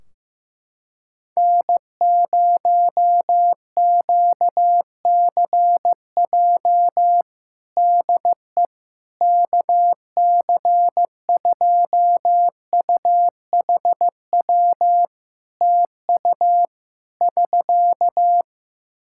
Morse code, heard as a keyed tone, about 16 wpm, 700 Hz. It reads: N0QCJ DE KC2UHW TU <SK>
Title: K1USN SST QSO Practice 15wpm